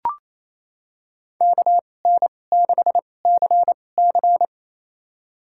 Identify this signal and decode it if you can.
Morse code, heard as a keyed tone, about 28 wpm, spaced as for 20 wpm, 700 Hz. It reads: KN6CC
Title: USA Call Signs - 5 Characters - 2x Character Spacing - 28wpm